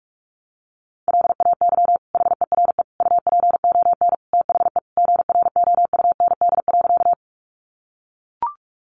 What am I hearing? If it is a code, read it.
Morse code, heard as a keyed tone, about 44 wpm, 700 Hz. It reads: LAY HERE UPON THE GROUND.
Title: Sentences from Top 1100 Words 45wpm